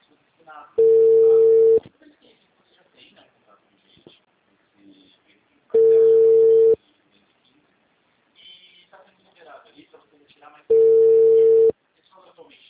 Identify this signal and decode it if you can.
Ringback tone (China, 450 Hz), with speech.
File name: out- -2010